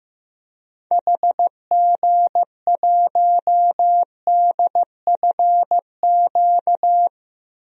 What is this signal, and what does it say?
Morse code, heard as a keyed tone, about 15 wpm, 700 Hz. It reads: HG1DFQ
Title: DXCC-100 Call Signs - 6 Characters - Repeated 3x - 15wpm